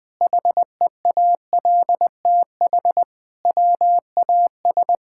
Morse code, heard as a keyed tone, about 20 wpm, 700 Hz. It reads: HEALTH WAS
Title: Sentences from Top 1000 Words - Review Part 6 - 20wpm